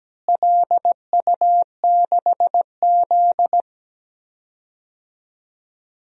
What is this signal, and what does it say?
Morse code, heard as a keyed tone, about 17 wpm, 700 Hz. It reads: LU6Z
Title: DXCC-100 Call Signs - 4 Characters - Repeated 2x - 17wpm